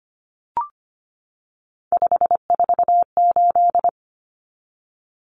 Morse code, heard as a keyed tone, about 25 wpm, 700 Hz. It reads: 548
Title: Numbers - 3-Digit 25wpm